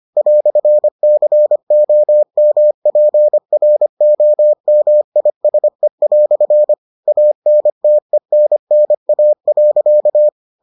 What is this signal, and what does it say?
Morse code, heard as a keyed tone, about 25 wpm, 600 Hz. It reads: "COMPROMISE" ANTENNA.